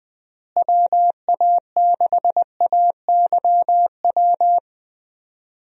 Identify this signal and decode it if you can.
Morse code, heard as a keyed tone, about 20 wpm, 700 Hz. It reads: WA6AYW
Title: USA Call Signs - 6 Characters - Repeated 2x - 20wpm